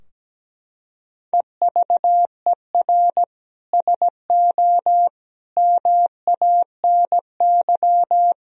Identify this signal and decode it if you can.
Morse code, heard as a keyed tone, about 17 wpm, 700 Hz. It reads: EVER SO MANY
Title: Sentences from Top 1000 Words - Review Part 3 - 17wpm